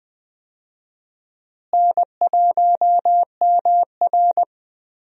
Morse code, heard as a keyed tone, about 21 wpm, 700 Hz. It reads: N1MR
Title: USA Call Signs - 4 Characters - Repeated 2x - 20wpm